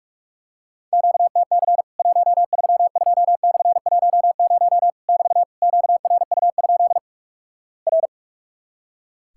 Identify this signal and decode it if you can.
Morse code, heard as a keyed tone, about 46 wpm, 700 Hz. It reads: QTC 132/10 <BT> QRU? R
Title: Worked All Europe DX Contest - QTC Exchange - 45wpm